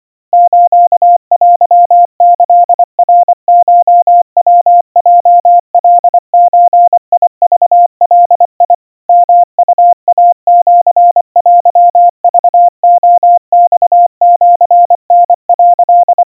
Morse code (Japanese Wabun code), heard as a keyed tone, about 25 wpm, 700 Hz. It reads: ステキナコヤヲカゾクガ ヨウイシテクレマシタ」
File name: 90 cpm 700 Hz Crispy